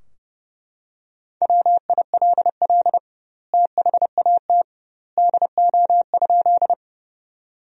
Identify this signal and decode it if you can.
Morse code, heard as a keyed tone, about 30 wpm, 700 Hz. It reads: WILL THAT DO?
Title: 2x Word-Spacing - Sentences from Top 100 Words - 30wpm